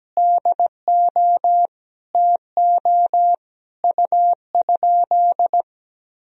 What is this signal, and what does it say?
Morse code, heard as a keyed tone, about 17 wpm, 700 Hz. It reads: DO TO U?
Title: Sentences from Top 100 Words and Abbreviations 17wpm